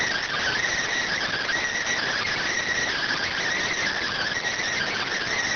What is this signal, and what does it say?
Emergency Alert System (SAME) tones, heard as harsh buzzing bursts. It is heard from 1.1 to 5.1 s.